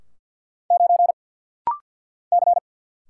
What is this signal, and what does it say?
Morse code, heard as a keyed tone, about 49 wpm, 700 Hz. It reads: 9 C
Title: Single Letter-Number Alt - Rapid-Fire 50wpm